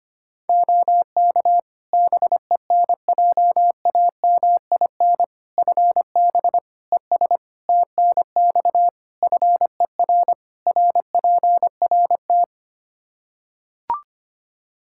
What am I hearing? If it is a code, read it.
Morse code, heard as a keyed tone, about 25 wpm, 700 Hz. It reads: OK BENJAMIN FB ES TNX FER RPRT
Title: K1USN SST QSO Practice 25wpm